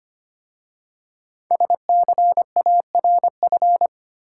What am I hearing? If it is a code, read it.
Morse code, heard as a keyed tone, about 25 wpm, 700 Hz. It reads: SCARF
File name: practice / 25wpm / Difficult Words with Hints - 25wpm